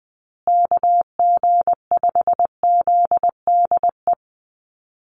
Morse code, heard as a keyed tone, about 20 wpm, 700 Hz. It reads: KG5ZDE
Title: USA Call Signs Repeated 3x 20wpm